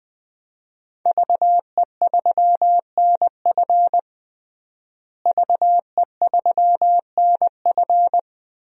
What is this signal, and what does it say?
Morse code, heard as a keyed tone, about 20 wpm, 700 Hz. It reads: VE3NF VE3NF
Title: DXCC-100 Call Signs - 5 Characters - Repeated 2x - 20wpm